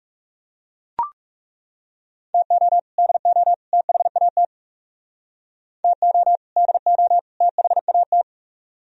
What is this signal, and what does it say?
Morse code, heard as a keyed tone, about 38 wpm, 700 Hz. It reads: TO DO THAT TO DO THAT